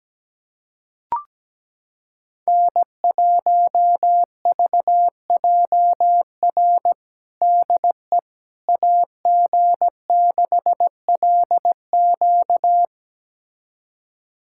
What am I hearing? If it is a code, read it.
Morse code, heard as a keyed tone, about 17 wpm, 700 Hz. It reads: N1VJR DE AG6LQ
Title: USA Callsign DE Callsign - 5 Characters - 17wpm